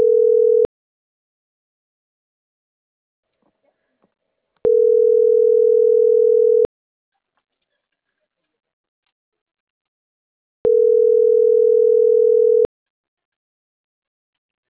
Ringback tone (North America), 440 + 480 Hz.